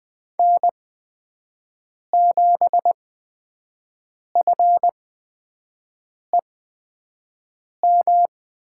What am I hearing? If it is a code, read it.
Morse code, heard as a keyed tone, about 20 wpm, spaced as for 5 wpm, 700 Hz. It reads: N7FEM
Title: USA Call Signs - 5 Characters - 8x Character Spacing - 20wpm